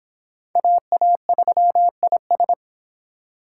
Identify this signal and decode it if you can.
Morse code, heard as a keyed tone, about 25 wpm, 700 Hz. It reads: AA3IS